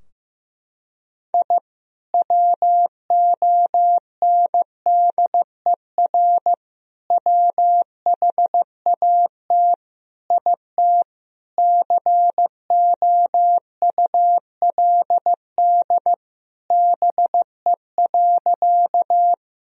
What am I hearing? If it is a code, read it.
Morse code, heard as a keyed tone, about 15 wpm, 700 Hz. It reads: I WONDER WHAT IT COULD BE.
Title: Sentences from Top 1000 Words - Review Part 3 - 15wpm